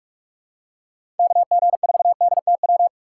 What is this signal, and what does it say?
Morse code, heard as a keyed tone, about 46 wpm, 700 Hz. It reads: KG4DTW
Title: USA Call Signs - 6 Characters - 2x Character Spacing - 45wpm